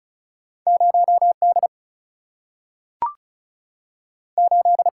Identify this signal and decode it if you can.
Morse code, heard as a keyed tone, about 36 wpm, 700 Hz. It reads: 0D 8
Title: Single-Letter Single-Number 35wpm